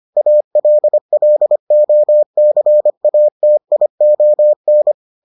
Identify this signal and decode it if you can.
Morse code, heard as a keyed tone, about 25 wpm, 600 Hz. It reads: ALLOCATION